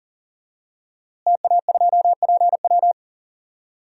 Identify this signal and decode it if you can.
Morse code, heard as a keyed tone, about 40 wpm, 700 Hz. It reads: TA2PW
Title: DXCC-100 Call Signs - 5 Characters - Repeated 2x - 40wpm